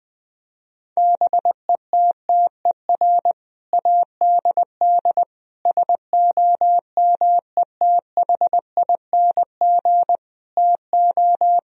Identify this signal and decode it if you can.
Morse code, heard as a keyed tone, about 20 wpm, 700 Hz. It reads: BETTER ADD SOMETHING TO
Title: Sentences from Top 1000 Words - Review Part 1 - 20wpm